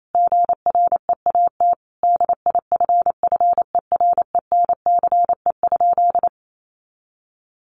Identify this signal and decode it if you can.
Morse code, heard as a keyed tone, about 28 wpm, 700 Hz. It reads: GREAT DIFFERENCE?
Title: Sentences from Top 800 Words 28wpm